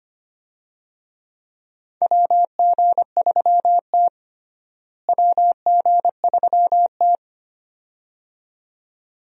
Morse code, heard as a keyed tone, about 25 wpm, 700 Hz. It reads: WG3T WG3T
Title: USA Call Signs - 4 Characters - Repeated 2x - 25wpm